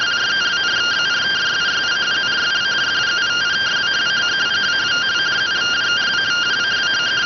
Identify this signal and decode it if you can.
Two-tone teleprinter signal, about 170 Hz shift.